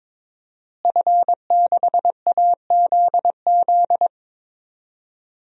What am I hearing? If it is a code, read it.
Morse code, heard as a keyed tone, about 22 wpm, 700 Hz. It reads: F6AZZ
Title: DXCC-100 Call Signs - 5 Characters - Repeated 3x - 22wpm